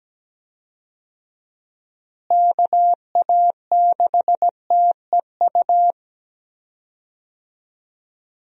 Morse code, heard as a keyed tone, about 17 wpm, 700 Hz. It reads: KA6TEU
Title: USA Call Signs - 6 Characters - Repeated 2x - 17wpm